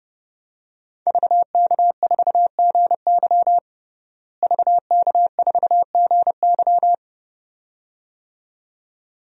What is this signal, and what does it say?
Morse code, heard as a keyed tone, about 30 wpm, 700 Hz. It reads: VK4GY VK4GY
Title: DXCC-100 Call Signs - 5 Characters - Repeated 2x - 30wpm